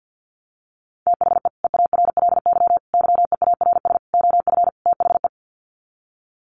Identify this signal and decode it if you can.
Morse code, heard as a keyed tone, about 49 wpm, 700 Hz. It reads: THE EARLY YEARS OF THE
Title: Sets of 5 Words Encore 50wpm